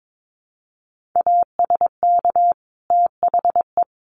Morse code, heard as a keyed tone, about 22 wpm, 700 Hz. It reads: ASK THE
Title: Sentences from Top 1500 Words 22wpm